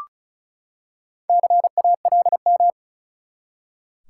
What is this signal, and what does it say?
Morse code, heard as a keyed tone, about 34 wpm, 700 Hz. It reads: CALM
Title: Four Letter Words - 35wpm